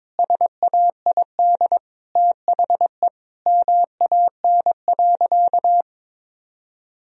Morse code, heard as a keyed tone, about 23 wpm, 700 Hz. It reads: SAID THE MAN.
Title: Sentences from Top 1000 Words - Review Part 3 - 22wpm